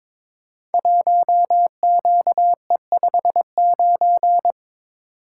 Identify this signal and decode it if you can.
Morse code, heard as a keyed tone, about 22 wpm, 700 Hz. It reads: 1QE59
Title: Code Groups - 5 Letter:Numbers - 22wpm